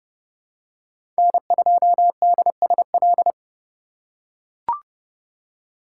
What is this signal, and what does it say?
Morse code, heard as a keyed tone, about 30 wpm, 700 Hz. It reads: N2DSL
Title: USA Call Signs - 5 Characters - 2x Character Spacing - 30wpm